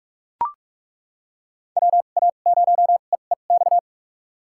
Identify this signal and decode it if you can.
Morse code, heard as a keyed tone, about 42 wpm, spaced as for 31 wpm, 700 Hz. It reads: WA0EEX
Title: USA Call Signs - 6 Characters - 2x Character Spacing - 45wpm